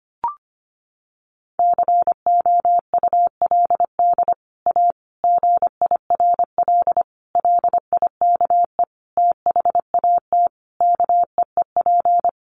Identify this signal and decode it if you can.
Morse code, heard as a keyed tone, about 25 wpm, 700 Hz. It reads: COULD A GIRL LIKE THAT KEEP
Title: Sentences from Top 1200 Words 25wpm